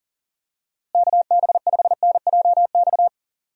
Morse code, heard as a keyed tone, about 40 wpm, 700 Hz. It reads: KB5NJX
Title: USA Call Signs - 6 Characters - Repeated 2x - 40wpm